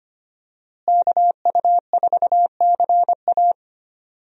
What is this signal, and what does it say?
Morse code, heard as a keyed tone, about 25 wpm, 700 Hz. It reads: KU4CA
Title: USA Call Signs - 5 Characters - Repeated 3x - 25wpm